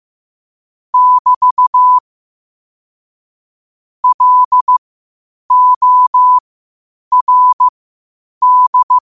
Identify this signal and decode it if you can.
Morse code, heard as a keyed tone, about 15 wpm, spaced as for 8 wpm, 1000 Hz. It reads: <BT> LORD
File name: qotdfiles / 8wpm